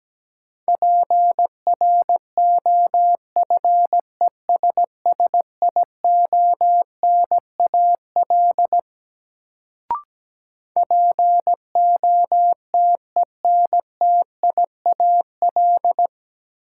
Morse code, heard as a keyed tone, about 17 wpm, 700 Hz. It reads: PROFESSIONAL POTENTIAL
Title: Sentences from Top 1000 Words - Review Part 4 - New Word Drill - 17wpm